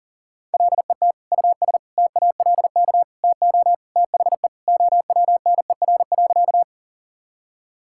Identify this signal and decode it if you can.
Morse code, heard as a keyed tone, about 40 wpm, 700 Hz. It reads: LET US TALK TO THE OWNER.
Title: Sentences from Top 1600 Words 40wpm